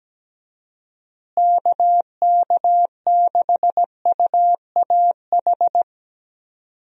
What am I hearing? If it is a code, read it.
Morse code, heard as a keyed tone, about 17 wpm, 700 Hz. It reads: KK6UAH
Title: USA Call Signs - 6 Characters - Repeated 3x - 17wpm